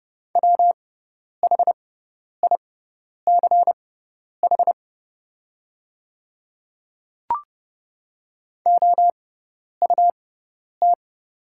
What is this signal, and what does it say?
Morse code, heard as a keyed tone, about 30 wpm, spaced as for 10 wpm, 700 Hz. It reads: WHICH OUT
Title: 6x Character Spacing - Top 300 Words - 30wpm